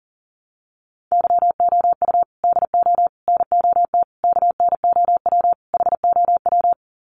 Morse code, heard as a keyed tone, about 40 wpm, 700 Hz. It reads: YOU DO NOT KNOW HOW
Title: Sets of 5 Words Encore 40wpm